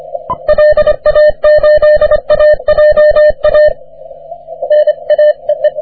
Morse code, heard as a keyed tone, about 26 wpm, 600 Hz. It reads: LA8AJA N A EE
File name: wp / k4m 40m cw 17 oct 2009 0729z